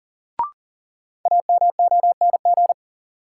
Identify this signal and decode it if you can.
Morse code, heard as a keyed tone, about 40 wpm, 700 Hz. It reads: AMONG
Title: Sentences from Top 500 Words - New Word Drill - 40wpm